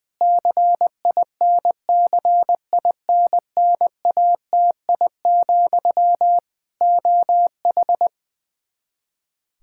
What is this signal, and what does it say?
Morse code, heard as a keyed tone, about 20 wpm, 700 Hz. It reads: CINCINNATI, OH
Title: Practice QSO - Exchange 1 - Beginning - 20wpm